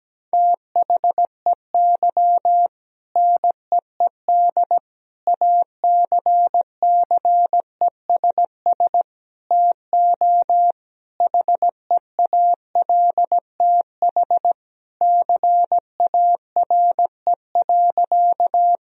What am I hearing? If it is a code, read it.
Morse code, heard as a keyed tone, about 17 wpm, 700 Hz. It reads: THEY NEED ACCESS TO HEALTH CARE.
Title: Sentences from Top 1000 Words - Review Part 4 - 17wpm